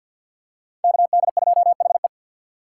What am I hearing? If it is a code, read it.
Morse code, heard as a keyed tone, about 46 wpm, 700 Hz. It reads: KD2HE
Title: K1USN SST QSO Practice - 50wpm